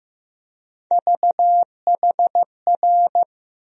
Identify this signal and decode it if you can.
Morse code, heard as a keyed tone, about 15 wpm, 700 Hz. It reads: VHR
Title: USA Call Sign Postfixes - 3 Letters 15wpm